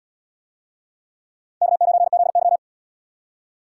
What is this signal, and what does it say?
Morse code, heard as a keyed tone, about 77 wpm, 700 Hz. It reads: K9ZY